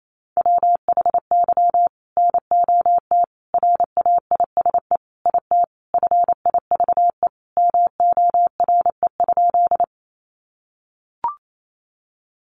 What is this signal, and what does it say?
Morse code, heard as a keyed tone, about 28 wpm, 700 Hz. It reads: WHY NOT RAISE IT FIVE MORE?
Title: Sentences from Top 1400 Words 28wpm